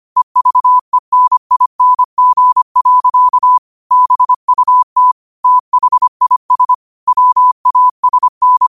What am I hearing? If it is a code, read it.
Morse code, heard as a keyed tone, about 25 wpm, 1000 Hz. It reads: EVENING. BUT THIS WASN
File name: qotdfiles / 25wpm